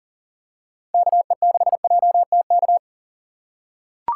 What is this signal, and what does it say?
Morse code, heard as a keyed tone, about 40 wpm, 700 Hz. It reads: KE6JTK
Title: USA Call Signs - 6 Characters - 2x Character Spacing - 40wpm